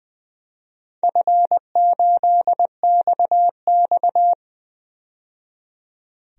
Morse code, heard as a keyed tone, about 20 wpm, 700 Hz. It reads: F8XX
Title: DXCC-100 Call Signs - 4 Characters - 20wpm